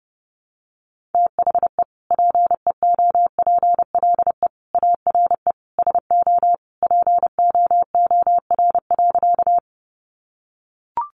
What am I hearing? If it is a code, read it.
Morse code, heard as a keyed tone, about 30 wpm, 700 Hz. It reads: THE PEOPLE ARE SO POOR.
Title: Sentences from Top 1000 Words 30wpm